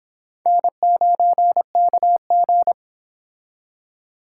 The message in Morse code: N9KG

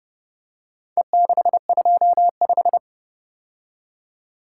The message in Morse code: E625

